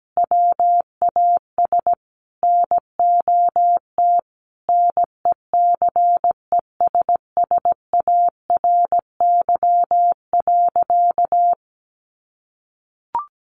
Morse code, heard as a keyed tone, about 17 wpm, 700 Hz. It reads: WAS NOT NECESSARY.
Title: Sentences from Top 800 Words 17wpm